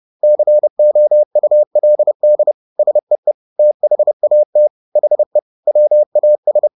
Morse code, heard as a keyed tone, about 30 wpm, 600 Hz. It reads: COULD SEE THAT HE WAS
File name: the old man and the sea / the old man and the sea 1 30wpm